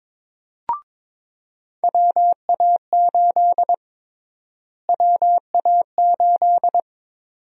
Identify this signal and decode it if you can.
Morse code, heard as a keyed tone, about 23 wpm, 700 Hz. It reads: WA8 WA8
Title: USA Call Sign Prefixes - Repeated 2x - 22wpm